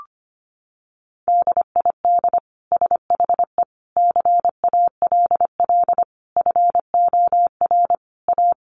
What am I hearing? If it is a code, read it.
Morse code, heard as a keyed tone, about 25 wpm, 700 Hz. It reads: DID SHE CALL FOR A
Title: Sentences from Top 1300 Words 25wpm